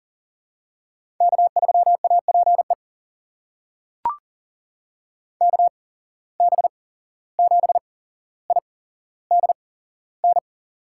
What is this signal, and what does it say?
Morse code, heard as a keyed tone, about 38 wpm, spaced as for 16 wpm, 700 Hz. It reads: K3APE KB7IDN
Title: USA Call Signs - 8x Character Spacing - 40wpm